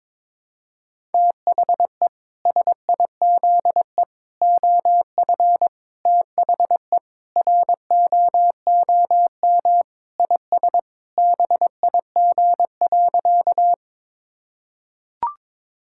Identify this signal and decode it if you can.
Morse code, heard as a keyed tone, about 22 wpm, 700 Hz. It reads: THE SIZE OF THE ROOM IS BIG.
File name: practice / 22wpm / Sentences with 3-Word Phrases - 22wpm